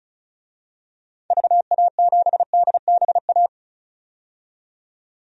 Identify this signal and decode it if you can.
Morse code, heard as a keyed tone, about 36 wpm, 700 Hz. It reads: VA7DBA
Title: DXCC-100 Call Signs - 6 Characters - 35wpm